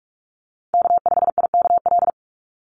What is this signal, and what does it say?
Morse code, heard as a keyed tone, about 44 wpm, 700 Hz. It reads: K5IKL
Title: USA Call Signs 45wpm